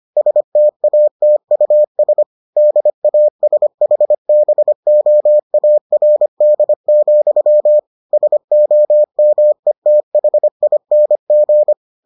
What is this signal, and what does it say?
Morse code, heard as a keyed tone, about 25 wpm, 600 Hz. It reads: STATUS DASHBOARD, SOMETHING